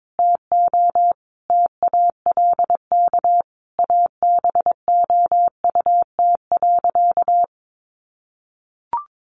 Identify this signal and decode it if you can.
Morse code, heard as a keyed tone, about 22 wpm, 700 Hz. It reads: TO TALK ABOUT.
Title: Sentences from Top 400 Words 22wpm